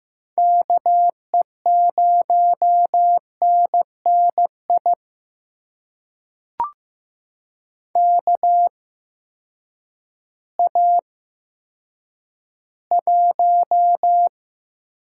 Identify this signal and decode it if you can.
Morse code, heard as a keyed tone, about 15 wpm, spaced as for 9 wpm, 700 Hz. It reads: KE0NNI KA1
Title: USA Call Signs - 6 Characters - 8x Character Spacing - 15wpm